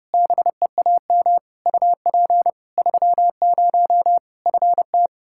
Morse code, heard as a keyed tone, about 32 wpm, 700 Hz. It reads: BEAM UP 30 FT
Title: Practice QSO - Exchange 2 - Info - 30wpm